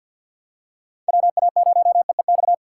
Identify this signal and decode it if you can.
Morse code, heard as a keyed tone, about 53 wpm, 700 Hz. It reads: WA0EEX
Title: USA Call Signs - 6 Characters - 6x Character Spacing - 50wpm